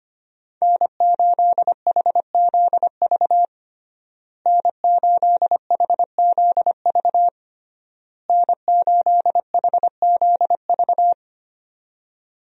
Morse code, heard as a keyed tone, about 25 wpm, 700 Hz. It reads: N8HZV N8HZV N8HZV
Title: USA Call Signs - 5 Characters - Repeated 3x - 25wpm